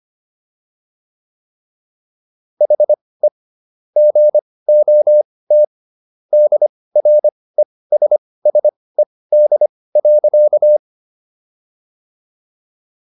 Morse code, heard as a keyed tone, about 25 wpm, spaced as for 18 wpm, 600 Hz. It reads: HE GOT DRESSED.